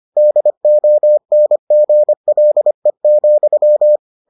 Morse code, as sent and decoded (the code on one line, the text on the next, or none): -.. --- -. --. .-.. --..--
DONGLE,